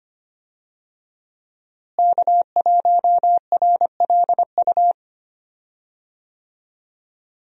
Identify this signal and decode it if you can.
Morse code, heard as a keyed tone, about 25 wpm, 700 Hz. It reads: K1RLU
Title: USA Call Signs - 5 Characters - Repeated 2x - 25wpm